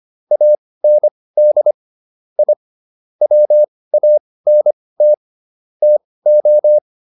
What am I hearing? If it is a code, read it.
Morse code, heard as a keyed tone, about 25 wpm, spaced as for 18 wpm, 600 Hz. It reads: AND I WANT TO